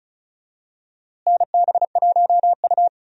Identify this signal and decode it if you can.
Morse code, heard as a keyed tone, about 36 wpm, 700 Hz. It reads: NB1U
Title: USA Call Signs - 4 Characters - Repeated 2x - 35wpm